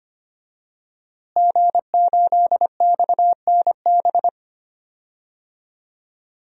Morse code, heard as a keyed tone, about 25 wpm, 700 Hz. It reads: G8XNB
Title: DXCC-100 Call Signs - 5 Characters - Repeated 2x - 25wpm